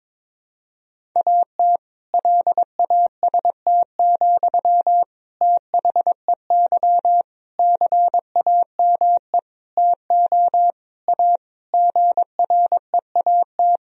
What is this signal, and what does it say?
Morse code, heard as a keyed tone, about 22 wpm, 700 Hz. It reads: AT LAST, THEY CAME TO A GREAT
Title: Sentences from Top 1000 Words - Review Part 1 - 22wpm